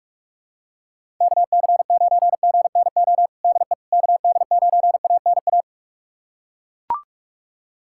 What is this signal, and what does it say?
Morse code, heard as a keyed tone, about 41 wpm, 700 Hz. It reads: KC9GNO DE KD9ANA
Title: USA Callsign DE Callsign - 6 Characters - 45wpm